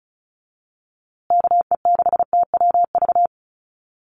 Morse code, heard as a keyed tone, about 34 wpm, 700 Hz. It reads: KE6TWV
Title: USA Call Signs Repeated 2x 35wpm